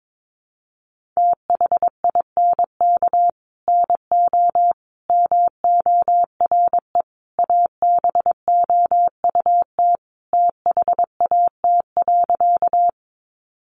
Morse code, heard as a keyed tone, about 22 wpm, 700 Hz. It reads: THINK NO MORE ABOUT THAT.
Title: Sentences from Top 200 Words 22wpm